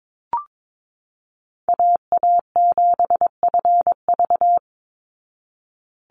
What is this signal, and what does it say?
Morse code, heard as a keyed tone, about 22 wpm, 700 Hz. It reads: AA7FV
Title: Call Signs Repeated 3x - 22wpm